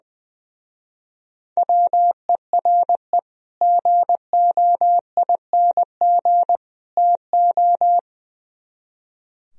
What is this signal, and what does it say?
Morse code, heard as a keyed tone, about 23 wpm, 700 Hz. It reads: WERE GOING TO
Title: Speed-Racing - Sets of 3 Words 20wpm